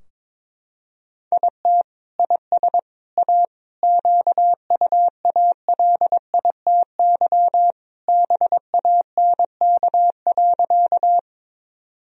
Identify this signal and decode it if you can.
Morse code, heard as a keyed tone, about 22 wpm, 700 Hz. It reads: IT IS A QUALITY BANK.
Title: Sentences from Top 1000 Words - Review Part 4 - 22wpm